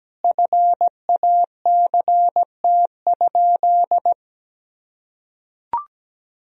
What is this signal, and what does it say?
Morse code, heard as a keyed tone, about 17 wpm, 700 Hz. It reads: FACT?
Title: Sentences from Top 1000 Words - Review Part 4 - 17wpm